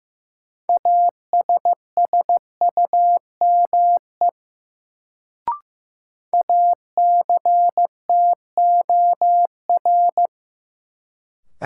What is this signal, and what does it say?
Morse code, heard as a keyed tone, about 15 wpm, 700 Hz. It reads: ASSUME ACTOR
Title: Sentences from Top 1500 Words - New Word Drill - 15wpm